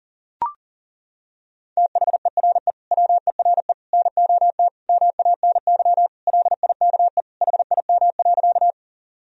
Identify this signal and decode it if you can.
Morse code, heard as a keyed tone, about 40 wpm, 700 Hz. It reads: THERE WERE NOT MANY LIKE HIM.
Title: Sentences from Top 1000 Words - Review Part 1 - 40wpm